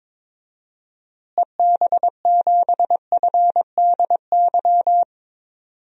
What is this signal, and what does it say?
Morse code, heard as a keyed tone, about 22 wpm, 700 Hz. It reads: EB7FDY
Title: DXCC-100 Call Signs - 6 Characters - Repeated 3x - 22wpm